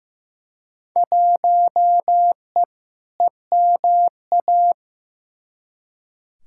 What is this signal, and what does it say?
Morse code, heard as a keyed tone, about 15 wpm, 700 Hz. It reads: 1E EMA
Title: Field Day Exchanges Encore - 15wpm